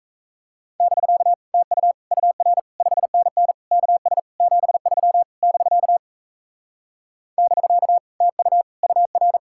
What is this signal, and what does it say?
Morse code, heard as a keyed tone, about 37 wpm, 700 Hz. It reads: <BK> TU UR 5NN KS 73 <BK> <BK> TU UR